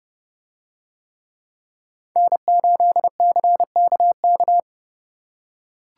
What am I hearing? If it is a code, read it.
Morse code, heard as a keyed tone, about 28 wpm, 700 Hz. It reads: N8CKK